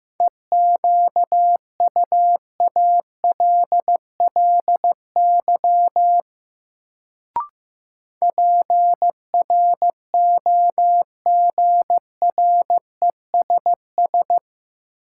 Morse code, heard as a keyed tone, about 15 wpm, 700 Hz. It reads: EQUALLY PROGRESS
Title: Sentences from Top 1400 Words - New Word Drill - 15wpm